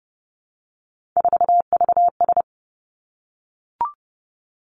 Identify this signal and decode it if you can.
Morse code, heard as a keyed tone, about 30 wpm, 700 Hz. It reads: 4VS
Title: Three Challenging Letter-Number Combinations 30wpm